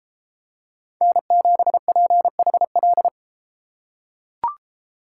Morse code, heard as a keyed tone, about 32 wpm, 700 Hz. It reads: N7PHL